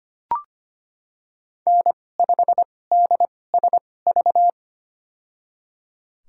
Morse code, heard as a keyed tone, about 25 wpm, spaced as for 18 wpm, 700 Hz. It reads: N5DSV
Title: USA Call Signs - 5 Characters - 2x Character Spacing - 25wpm